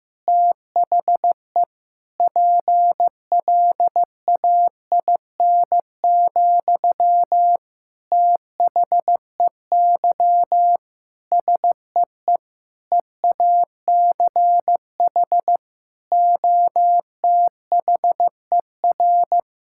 Morse code, heard as a keyed tone, about 15 wpm, 700 Hz. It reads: THE PLAIN, THEY SEE EACH OTHER
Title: Sentences from Top 1000 Words - Review Part 5 - 15wpm